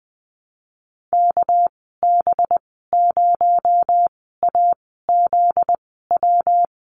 Morse code, heard as a keyed tone, about 20 wpm, spaced as for 14 wpm, 700 Hz. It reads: KB0AZW
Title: 245 Lesson 39g - Letter z - Call Signs 20wpm